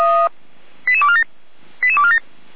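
Five-tone selective call (ZVEI), heard as quick runs of five tones, tones of about 70 ms. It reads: 88247 88247
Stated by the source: ZVEI five-tone selective call 88249 received by SDR, logged by the Feuerwehr Orlamünde fire brigade.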